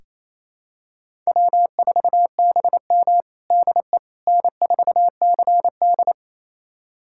Morse code, heard as a keyed tone, about 27 wpm, 700 Hz. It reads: W4BM DE N4CD
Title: USA Callsign DE Callsign - 4 Characters - 28wpm